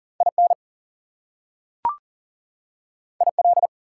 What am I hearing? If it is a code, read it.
Morse code, heard as a keyed tone, about 40 wpm, 700 Hz. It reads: IN IL
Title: US States and Canadian Provinces 40wpm